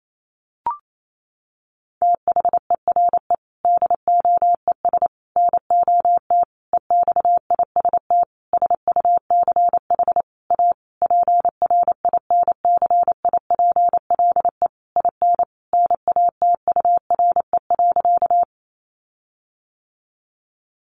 Morse code, heard as a keyed tone, about 28 wpm, 700 Hz. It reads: THERE DOES NOT EXIST SUCH A PRINCIPLE IN NATURE.
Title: Sentences from Top 1400 Words 28wpm